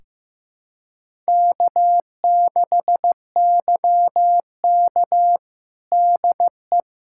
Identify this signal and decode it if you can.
Morse code, heard as a keyed tone, about 15 wpm, 700 Hz. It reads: K6YK DE
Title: USA Callsign DE Callsign - 4 Characters - 15wpm